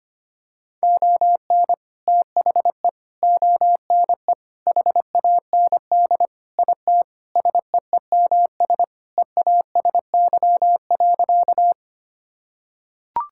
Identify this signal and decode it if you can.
Morse code, heard as a keyed tone, about 25 wpm, 700 Hz. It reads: ON THE ONE HAND IT SEEMS EASY.
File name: practice / 25wpm / Sentences with 4-Word Phrases - 25wpm